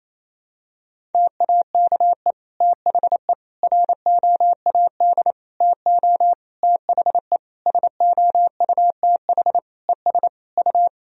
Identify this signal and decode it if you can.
Morse code, heard as a keyed tone, about 27 wpm, 700 Hz. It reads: TAKE THE ROAD TO THE SOUTH ES U
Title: Sentences from Top 400 Words and Abbreviations 28wpm